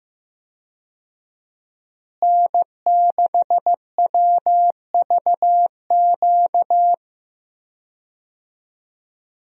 Morse code, heard as a keyed tone, about 15 wpm, 700 Hz. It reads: N6WVQ
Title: USA Call Signs - 5 Characters - Repeated 3x - 15wpm